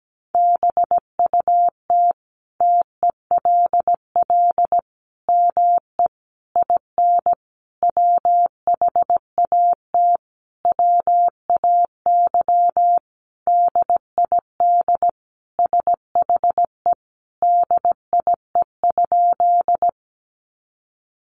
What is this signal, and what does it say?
Morse code, heard as a keyed tone, about 17 wpm, 700 Hz. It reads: BUT TELL ME IN WHAT WAY DID SHE DIE?
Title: Sentences from Top 1300 Words 17wpm